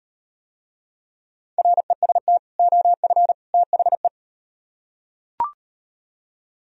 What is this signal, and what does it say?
Morse code, heard as a keyed tone, about 36 wpm, 700 Hz. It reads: REST OF THE